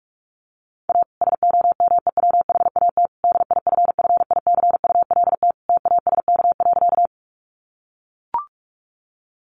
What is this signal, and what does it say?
Morse code, heard as a keyed tone, about 44 wpm, 700 Hz. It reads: A SOMEWHAT DIFFICULT TASK.
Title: Sentences from Top 1300 Words 45wpm